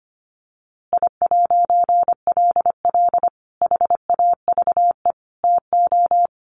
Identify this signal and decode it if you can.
Morse code, heard as a keyed tone, about 25 wpm, 700 Hz. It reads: I'LL HAVE TO
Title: A Sound Of Thunder 25wpm